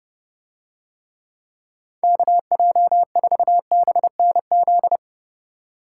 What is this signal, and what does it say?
Morse code, heard as a keyed tone, about 29 wpm, 700 Hz. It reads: KJ4BNZ